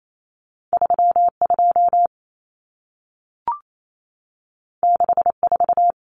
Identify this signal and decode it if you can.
Morse code, heard as a keyed tone, about 28 wpm, 700 Hz. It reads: 32 64
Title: Common Cut Numbers - 2-Digit 28wpm